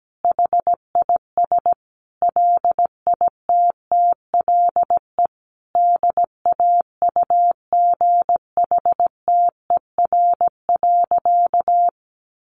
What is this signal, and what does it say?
Morse code, heard as a keyed tone, about 17 wpm, 700 Hz. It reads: HIS LITTLE DAUGHTER.
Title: Sentences from Top 1500 Words 17wpm